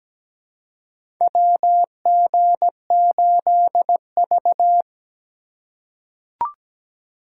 Morse code, heard as a keyed tone, about 17 wpm, 700 Hz. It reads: WG8V
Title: USA Call Signs - 4 Characters - 4x Character Spacing - 17wpm